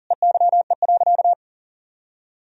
Morse code, heard as a keyed tone, about 40 wpm, 700 Hz. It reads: EYE.
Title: Sentences from Top 1000 Words - Review Part 1 - 40wpm